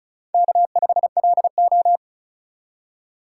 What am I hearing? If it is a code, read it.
Morse code, heard as a keyed tone, about 36 wpm, 700 Hz. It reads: K5LO
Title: USA Call Signs - 3x Character Spacing - 35wpm